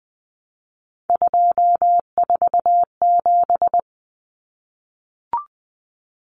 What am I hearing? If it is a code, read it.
Morse code, heard as a keyed tone, about 20 wpm, 700 Hz. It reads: 247
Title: Numbers - 3-Digit 20wpm